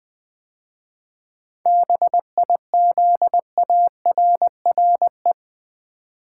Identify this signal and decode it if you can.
Morse code, heard as a keyed tone, about 20 wpm, 700 Hz. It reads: BIZARRE
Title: 7 Letter Words - First 3 Letters Given - 20wpm